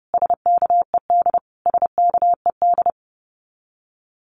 Morse code, heard as a keyed tone, about 30 wpm, 700 Hz. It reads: SKED SKED
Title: Abbreviations Repeated 3x - 30wpm